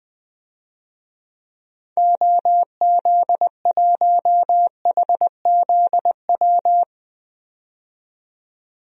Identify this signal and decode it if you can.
Morse code, heard as a keyed tone, about 21 wpm, 700 Hz. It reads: OZ1HZW
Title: DXCC-100 Call Signs - 6 Characters - 20wpm